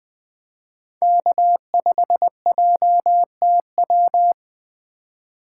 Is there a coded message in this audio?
Yes: Morse code from 1.0 to 4.3 s, heard as a keyed tone, about 19 wpm, 700 Hz. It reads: K5JTW